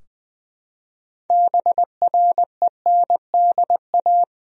Morse code, heard as a keyed tone, about 20 wpm, 700 Hz. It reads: BRENDA
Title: USA Common Names - 20wpm